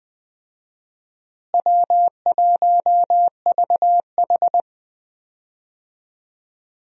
Morse code, heard as a keyed tone, about 20 wpm, 700 Hz. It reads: W1VH
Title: USA Call Signs - 4 Characters - Repeated 3x - 20wpm